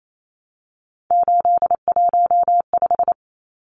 Morse code, heard as a keyed tone, about 28 wpm, 700 Hz. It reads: 815
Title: Numbers - 3-Digit 28wpm